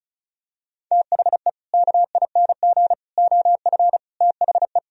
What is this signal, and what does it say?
Morse code, heard as a keyed tone, about 36 wpm, 700 Hz. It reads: THE KING OF THE
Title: Sentences from Top 1000 Words - Review Part 2 - 35wpm